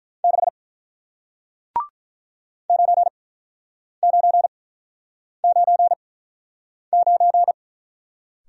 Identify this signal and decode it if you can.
Morse code, heard as a keyed tone, about 46 wpm, spaced as for 9 wpm, 700 Hz. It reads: 6 9999